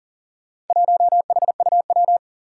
Morse code, heard as a keyed tone, about 40 wpm, 700 Hz. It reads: 1HUW
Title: Four Challenging Letter-Number Combinations 40wpm